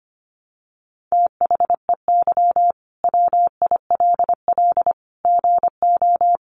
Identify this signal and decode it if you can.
Morse code, heard as a keyed tone, about 25 wpm, 700 Hz. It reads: THEY WILL GO
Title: Sentences from Top 1100 Words 25wpm